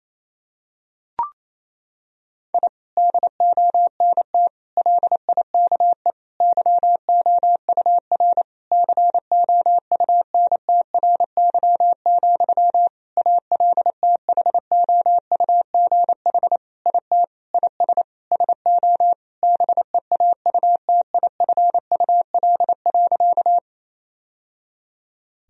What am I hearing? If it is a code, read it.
Morse code, heard as a keyed tone, about 27 wpm, 700 Hz. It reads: I DONT LIKE YOUR COUNTRY, ALTHOUGH IT IS SO BEAUTIFUL.
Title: Sentences from Top 1000 Words - Review Part 3 - 28wpm